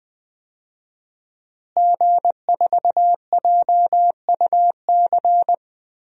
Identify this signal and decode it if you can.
Morse code, heard as a keyed tone, about 20 wpm, 700 Hz. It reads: G4JUC
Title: DXCC-100 Call Signs - 5 Characters - Repeated 3x - 20wpm